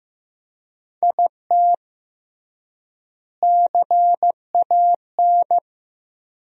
Morse code, heard as a keyed tone, about 16 wpm, 700 Hz. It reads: IT CAN
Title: 3x Word-Spacing - Sentences from Top 400 Words - 15wpm